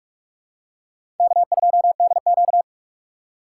Morse code, heard as a keyed tone, about 46 wpm, 700 Hz. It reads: K2DQ
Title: USA Call Signs - 4 Characters - 3x Character Spacing - 45wpm